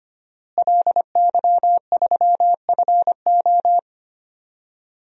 Morse code, heard as a keyed tone, about 25 wpm, 700 Hz. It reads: LY3FO
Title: DXCC-100 Call Signs - 5 Characters - Repeated 2x - 25wpm